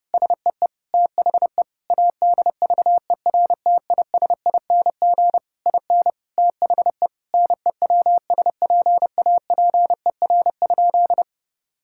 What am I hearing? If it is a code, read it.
Morse code, heard as a keyed tone, about 30 wpm, 700 Hz. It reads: SEE THE ADVERTISING IN THE NEWSPAPER?
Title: Sentences from Top 1000 Words - Review Part 6 - 30wpm